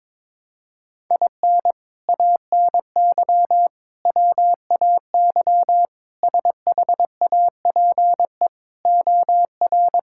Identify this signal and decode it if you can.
Morse code, heard as a keyed tone, about 22 wpm, 700 Hz. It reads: IN ANY WAY SHAPE OR
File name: practice / 22wpm / Sets of 6 Words - 22wpm